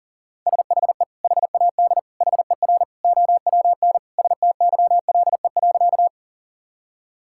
Morse code, heard as a keyed tone, about 40 wpm, 700 Hz. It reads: SHE HAD HER OWN STYLE.
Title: Sentences from Top 1000 Words - Review Part 5 - 40wpm